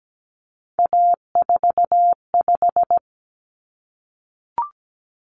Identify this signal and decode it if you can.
Morse code, heard as a keyed tone, about 17 wpm, 700 Hz. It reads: A45
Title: Common Cut Numbers - 3-Digit 17wpm